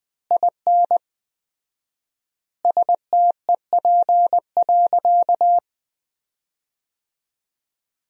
Morse code, heard as a keyed tone, about 20 wpm, 700 Hz. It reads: IN STEP.
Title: 4x Word-Spacing - Sentences from Top 400 Words - 20wpm